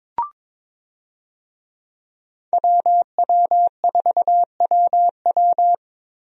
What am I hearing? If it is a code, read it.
Morse code, heard as a keyed tone, about 22 wpm, 700 Hz. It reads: WW4WW
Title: USA Call Signs - 5 Characters - Repeated 2x - 22wpm